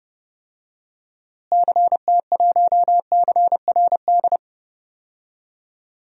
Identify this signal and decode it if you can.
Morse code, heard as a keyed tone, about 32 wpm, 700 Hz. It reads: CT1CRD
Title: DXCC-100 Call Signs - 6 Characters - Repeated 2x - 30wpm